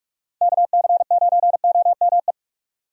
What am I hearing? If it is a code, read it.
Morse code, heard as a keyed tone, about 41 wpm, 700 Hz. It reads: KC9OME
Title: USA Call Signs - 6 Characters - 45wpm